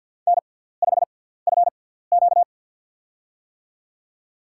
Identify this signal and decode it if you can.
Morse code, heard as a keyed tone, about 53 wpm, spaced as for 17 wpm, 700 Hz. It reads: N5FQ
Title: USA Call Signs - 6x Character Spacing - 50wpm